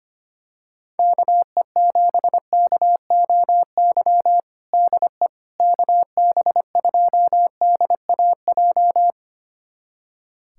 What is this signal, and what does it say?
Morse code, heard as a keyed tone, about 25 wpm, 700 Hz. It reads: KE7KOY DE KB2DAJ
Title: USA Callsign DE Callsign - 6 Characters - 25wpm